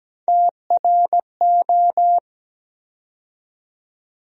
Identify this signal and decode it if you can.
Morse code, heard as a keyed tone, about 17 wpm, 700 Hz. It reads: TRO
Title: USA Call Sign Postfixes - 3 Letters 17wpm